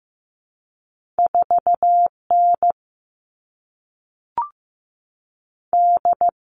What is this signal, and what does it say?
Morse code, heard as a keyed tone, about 15 wpm, 700 Hz. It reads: 4N D
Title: All Cut Numbers - 2-Digit 15wpm